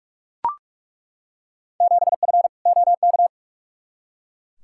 Morse code, heard as a keyed tone, about 43 wpm, 700 Hz. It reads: 7F OK
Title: Field Day Exchanges Encore - 45wpm